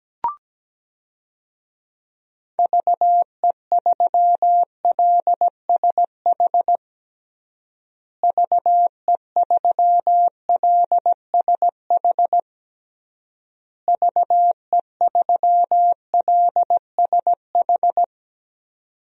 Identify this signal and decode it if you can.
Morse code, heard as a keyed tone, about 17 wpm, 700 Hz. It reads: VE3LSH VE3LSH VE3LSH
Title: DXCC-100 Call Signs - 6 Characters - Repeated 3x - 17wpm